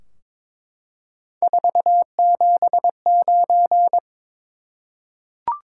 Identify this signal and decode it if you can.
Morse code, heard as a keyed tone, about 21 wpm, 700 Hz. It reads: 479
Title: Top 180 QSO Elements 22wpm